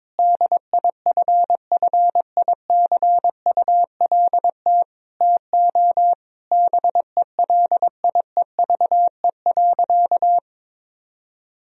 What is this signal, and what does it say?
Morse code, heard as a keyed tone, about 21 wpm, 700 Hz. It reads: DIFFICULT TO BELIEVE.
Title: Sentences from Top 1000 Words - Review Part 4 - 22wpm